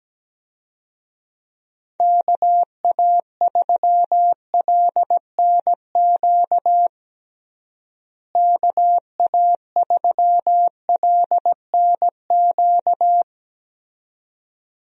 Morse code, heard as a keyed tone, about 17 wpm, 700 Hz. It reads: KA3LNQ KA3LNQ
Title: USA Call Signs - 6 Characters - Repeated 2x - 17wpm